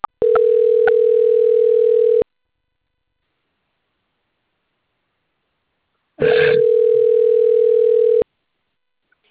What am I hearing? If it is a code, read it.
Ringback tone (North America), 440 + 480 Hz.
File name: rg-1600-unknown- - - .30165